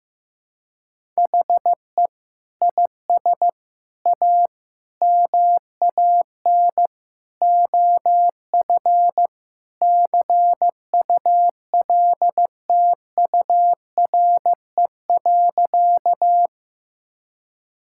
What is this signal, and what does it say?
Morse code, heard as a keyed tone, about 15 wpm, 700 Hz. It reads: HE IS A MAN OF CULTURE.
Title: Sentences from Top 1000 Words - Review Part 6 - 15wpm